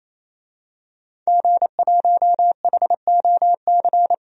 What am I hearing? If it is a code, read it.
Morse code, heard as a keyed tone, about 28 wpm, 700 Hz. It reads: G1HOC
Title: DXCC-100 Call Signs - 5 Characters - 28wpm